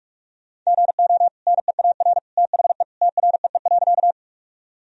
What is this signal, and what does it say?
Morse code, heard as a keyed tone, about 41 wpm, 700 Hz. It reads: GO NEAR THE TREE.
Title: Sentences from Top 1000 Words - Review Part 1 - 45wpm